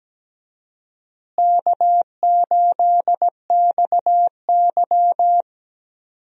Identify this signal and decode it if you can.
Morse code, heard as a keyed tone, about 17 wpm, 700 Hz. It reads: K8XY
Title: USA Call Signs - 4 Characters - Repeated 3x - 17wpm